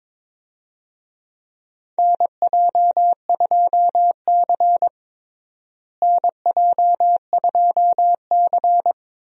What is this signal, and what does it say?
Morse code, heard as a keyed tone, about 22 wpm, 700 Hz. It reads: NJ2C NJ2C
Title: USA Call Signs - 4 Characters - Repeated 3x - 22wpm